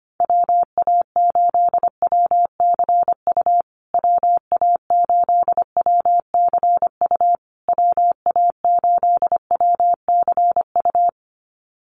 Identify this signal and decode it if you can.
Morse code, heard as a keyed tone, about 25 wpm, 700 Hz. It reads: WA8WCU WA8WCU WA8WCU
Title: USA Call Signs Repeated 3x 25wpm